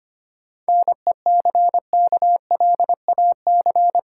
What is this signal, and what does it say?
Morse code, heard as a keyed tone, about 29 wpm, 700 Hz. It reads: NECKLAC
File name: practice / 25wpm / Difficult Words - Speed-Racing with Hints - 25wpm